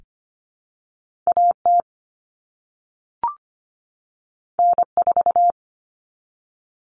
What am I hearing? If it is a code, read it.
Morse code, heard as a keyed tone, about 25 wpm, 700 Hz. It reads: AT N4
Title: All Cut Numbers - 2-Digit 25wpm